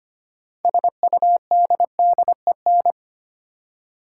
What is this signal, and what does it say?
Morse code, heard as a keyed tone, about 25 wpm, 700 Hz. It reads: SUDDEN
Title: Six Letter Words - 25wpm